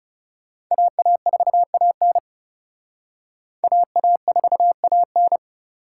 Morse code, heard as a keyed tone, about 38 wpm, 700 Hz. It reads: AA4AN AA4AN